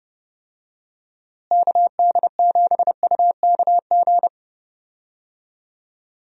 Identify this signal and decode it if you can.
Morse code, heard as a keyed tone, about 31 wpm, 700 Hz. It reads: KD7UKG
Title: USA Call Signs - 6 Characters - Repeated 2x - 30wpm